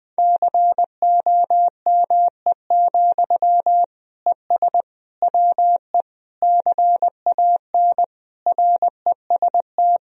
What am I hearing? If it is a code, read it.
Morse code, heard as a keyed tone, about 21 wpm, 700 Hz. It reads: COME, ES WE CAN REST
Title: Sentences from Top 500 Words and Abbreviations 20wpm